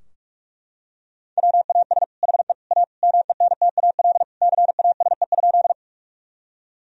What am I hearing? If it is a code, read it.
Morse code, heard as a keyed tone, about 41 wpm, 700 Hz. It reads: WAS HE A MENTAL CASE?
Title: Sentences from Top 1000 Words - Review Part 4 - 45wpm